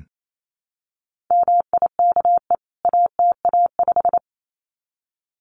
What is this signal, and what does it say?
Morse code, heard as a keyed tone, about 28 wpm, 700 Hz. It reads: MIKE ATA5
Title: CWT Exchange 28wpm